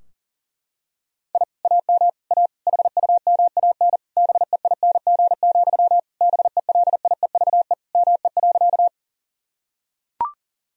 Morse code, heard as a keyed tone, about 38 wpm, 700 Hz. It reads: I AM A HUMAN BEING, BELIEVE ME.
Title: Sentences from Top 1000 Words - Review Part 3 - 40wpm